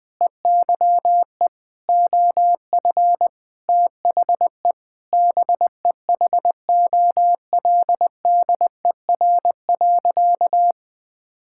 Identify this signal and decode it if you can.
Morse code, heard as a keyed tone, about 21 wpm, 700 Hz. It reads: EYE OF THE BEHOLDER.
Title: Sentences with 6-Word Phrases - 20wpm